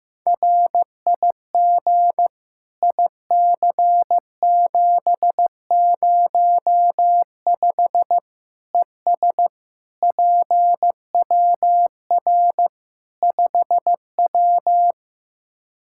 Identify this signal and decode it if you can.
Morse code, heard as a keyed tone, about 15 wpm, 700 Hz. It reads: RIG IC705 ES PWR 5W
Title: Practice QSO - Exchange 2 - Info - 15wpm